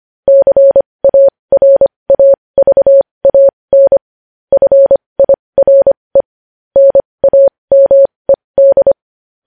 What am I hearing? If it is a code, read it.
Morse code, heard as a keyed tone, about 25 wpm, spaced as for 20 wpm, 550 Hz. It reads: CARAVAN FIRE NAMED
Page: news headlines in morse code at 20 wpm